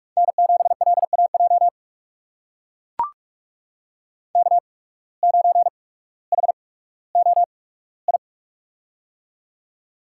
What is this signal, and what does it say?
Morse code, heard as a keyed tone, about 44 wpm, spaced as for 19 wpm, 700 Hz. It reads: N7LAJ K9HOI
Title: USA Call Signs - 5 Characters - 8x Character Spacing - 45wpm